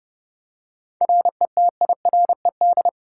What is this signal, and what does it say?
Morse code, heard as a keyed tone, about 30 wpm, 700 Hz. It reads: RETIRED
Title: Practice QSO - Exchange 3 - More Info - 30wpm